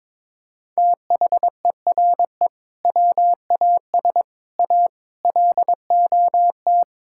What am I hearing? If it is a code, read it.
Morse code, heard as a keyed tone, about 22 wpm, 700 Hz. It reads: THERE WAS A LOT
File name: practice / 22wpm / Sentences with 4-Word Phrases - 22wpm